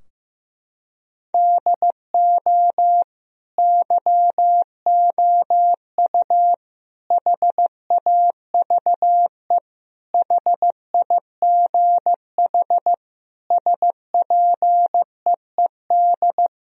Morse code, heard as a keyed tone, about 15 wpm, 700 Hz. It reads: DO YOU HAVE HIGH SPEED
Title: Sentences from Top 1000 Words - Review Part 4 - 15wpm